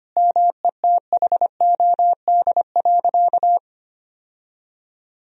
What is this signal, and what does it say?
Morse code, heard as a keyed tone, about 25 wpm, 700 Hz. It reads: METHOD.
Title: Sentences from Top 1000 Words - Review Part 3 - 25wpm